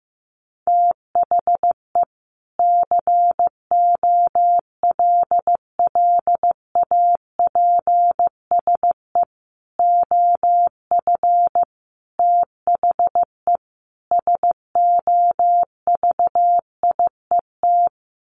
Morse code, heard as a keyed tone, about 15 wpm, 700 Hz. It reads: THE COLLAPSE OF THE SOVIET
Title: Sets of 5 Words Encore 15wpm